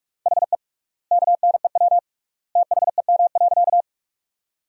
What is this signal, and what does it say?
Morse code, heard as a keyed tone, about 43 wpm, 700 Hz. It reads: HE KNEW THEM.
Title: 3x Word-Spacing - Sentences from Top 400 Words - 45wpm